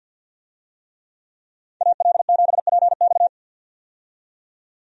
Morse code, heard as a keyed tone, about 52 wpm, 700 Hz. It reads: AL7PX
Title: USA Call Signs - 5 Characters - 50wpm